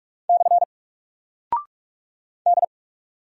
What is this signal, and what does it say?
Morse code, heard as a keyed tone, about 44 wpm, 700 Hz. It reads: / D
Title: Single Character (Letter:Number:Punctuation:Prosign) 45wpm